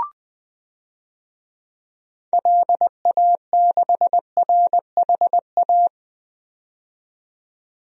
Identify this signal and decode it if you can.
Morse code, heard as a keyed tone, about 20 wpm, 700 Hz. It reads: LA6RHA
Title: DXCC-100 Call Signs - 6 Characters - 20wpm